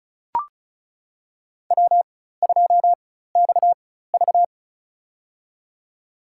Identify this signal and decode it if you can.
Morse code, heard as a keyed tone, about 36 wpm, spaced as for 16 wpm, 700 Hz. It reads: W2XV
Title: USA Call Signs - 4 Characters - 4x Character Spacing - 35wpm